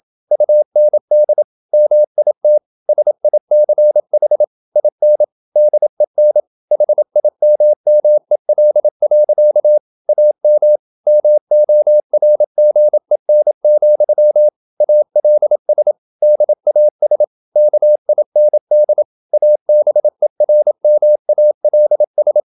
Morse code, heard as a keyed tone, about 28 wpm, 600 Hz. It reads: UND MIT SICH IN DEN HIMMEL. AM MORGEN, ALS DAS KIND ABERMALS